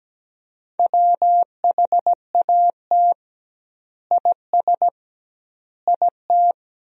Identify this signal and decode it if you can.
Morse code, heard as a keyed tone, about 17 wpm, 700 Hz. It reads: WHAT IS IT
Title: 2x Word-Spacing - Sentences from Top 100 Words - 17wpm